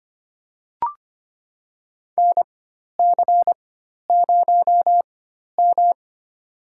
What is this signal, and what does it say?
Morse code, heard as a keyed tone, about 25 wpm, spaced as for 12 wpm, 700 Hz. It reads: NC0M
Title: USA Call Signs - 4 Characters - 4x Character Spacing - 25wpm